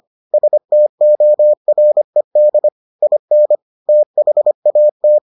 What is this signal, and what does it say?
Morse code, heard as a keyed tone, about 25 wpm, 600 Hz. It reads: STORED IN THAT